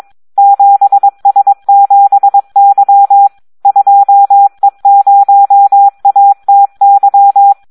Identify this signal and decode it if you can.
Morse code, heard as a keyed tone, about 22 wpm, 800 Hz. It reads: 7S7Y 2E0ATY
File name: Audio practica CW / SD 1 22